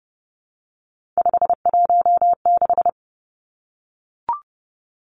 Morse code, heard as a keyed tone, about 30 wpm, 700 Hz. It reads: 516
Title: Three Challenging Letter-Number Combinations 30wpm